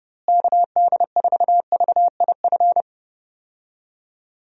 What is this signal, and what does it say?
Morse code, heard as a keyed tone, about 30 wpm, 700 Hz. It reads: KD4VIF
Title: USA Call Signs - 6 Characters - Repeated 3x - 30wpm